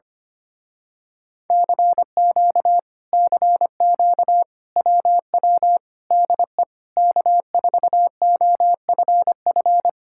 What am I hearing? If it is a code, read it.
Morse code, heard as a keyed tone, about 24 wpm, 700 Hz. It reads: CQ CQ WW DE K4OFF
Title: Speed-Racing Common QSO Phrases - 25wpm